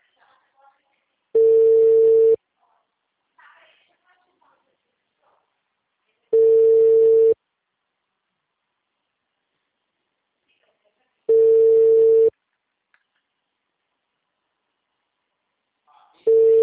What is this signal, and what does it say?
Ringback tone (China, 450 Hz), with speech.